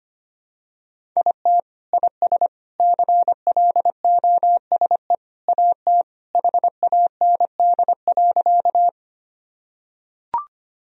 Morse code, heard as a keyed tone, about 25 wpm, 700 Hz. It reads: IT IS CLOSE AT HAND.
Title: Sentences from Top 1000 Words - Review Part 1 - 25wpm